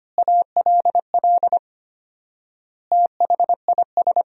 Morse code, heard as a keyed tone, about 25 wpm, 700 Hz. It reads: ALL THIS
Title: 4x Word-Spacing - Sentences from Top 200 Words - 25wpm